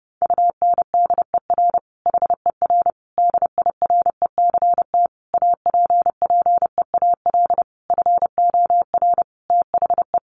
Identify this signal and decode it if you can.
Morse code, heard as a keyed tone, about 30 wpm, 700 Hz. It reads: UNDER HER DIRECT APPEAL FOR THE
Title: Sentences from Top 1300 Words 30wpm